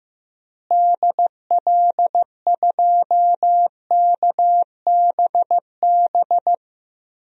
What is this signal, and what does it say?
Morse code, heard as a keyed tone, about 15 wpm, 700 Hz. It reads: DL2KBB
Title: DXCC-100 Call Signs - 6 Characters - Repeated 3x - 15wpm